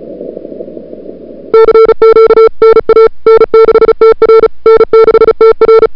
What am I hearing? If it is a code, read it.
Morse code, heard as a keyed tone, about 34 wpm, 440 Hz. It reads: CQ NA N6TR N6TR
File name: CWSprint / February 2007 CW Sprint / 01-2007 CW Sprint Hour1